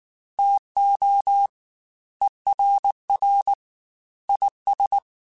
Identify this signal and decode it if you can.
Morse code, heard as a keyed tone, about 19 wpm, 800 Hz. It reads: TO ERR IS